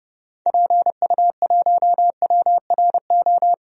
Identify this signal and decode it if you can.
Morse code, heard as a keyed tone, about 30 wpm, 700 Hz. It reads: PU1WRO
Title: DXCC-100 Call Signs - 6 Characters - 30wpm